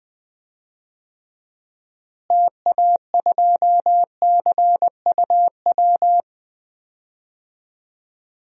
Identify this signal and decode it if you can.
Morse code, heard as a keyed tone, about 21 wpm, 700 Hz. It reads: TA2CUW
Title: DXCC-100 Call Signs - 6 Characters - Repeated 2x - 20wpm